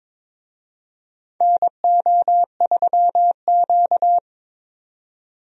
Morse code, heard as a keyed tone, about 22 wpm, 700 Hz. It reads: NO3Q
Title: USA Call Signs - 4 Characters - 22wpm